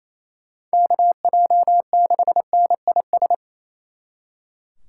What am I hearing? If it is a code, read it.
Morse code, heard as a keyed tone, about 34 wpm, 700 Hz. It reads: KJ6NIS
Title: Speed Racing - USA 2x3 Call Signs 28wpm